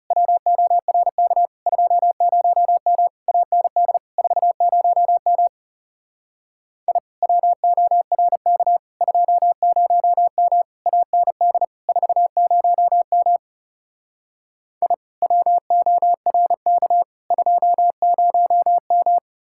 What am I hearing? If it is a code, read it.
Morse code, heard as a keyed tone, about 37 wpm, 700 Hz. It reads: WORK 20M AND 40M I WORK 20M AND 40M I WORK 20M
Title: Speed-Racing Common QSO Phrases - 30wpm